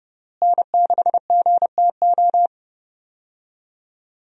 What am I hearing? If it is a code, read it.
Morse code, heard as a keyed tone, about 32 wpm, 700 Hz. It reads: N6GTO
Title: USA Call Signs - 5 Characters - Repeated 3x - 30wpm